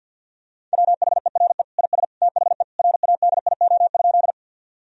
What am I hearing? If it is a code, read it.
Morse code, heard as a keyed tone, about 47 wpm, 700 Hz. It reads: WHERE IS THE RADIO?
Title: Sentences from Top 1000 Words - Review Part 3 - 50wpm